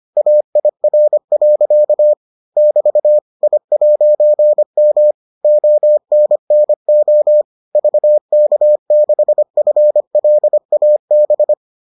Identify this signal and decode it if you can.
Morse code, heard as a keyed tone, about 25 wpm, 600 Hz. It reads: AIR. <BT> I'M ONNO VK6FLAB